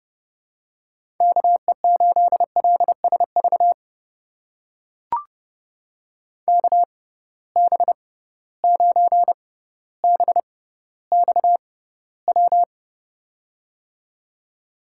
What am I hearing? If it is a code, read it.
Morse code, heard as a keyed tone, about 30 wpm, spaced as for 15 wpm, 700 Hz. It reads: KE8LSV KB9BXW
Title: USA Call Signs - 6 Characters - 6x Character Spacing - 30wpm